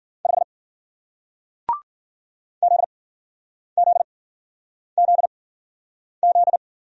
Morse code, heard as a keyed tone, about 50 wpm, spaced as for 9 wpm, 700 Hz. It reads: 5 ZZZZ
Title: Speed-Racing Single Letter-Number 40wpm